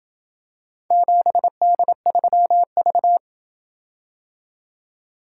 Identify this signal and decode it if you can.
Morse code, heard as a keyed tone, about 27 wpm, 700 Hz. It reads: 7D3V